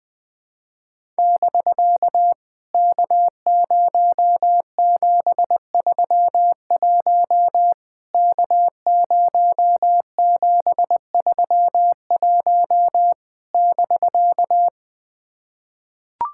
Morse code, heard as a keyed tone, about 19 wpm, 700 Hz. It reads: <BK> K0731 K0731 <BK>
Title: POTA QSO Practice - 20wpm